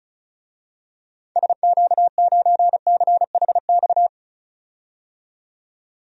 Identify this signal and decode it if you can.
Morse code, heard as a keyed tone, about 36 wpm, 700 Hz. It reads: SQ9CHX
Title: DXCC-100 Call Signs - 6 Characters - 35wpm